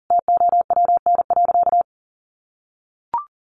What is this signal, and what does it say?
Morse code, heard as a keyed tone, about 40 wpm, 700 Hz. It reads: TOWN.
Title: Sentences from Top 1100 Words 40wpm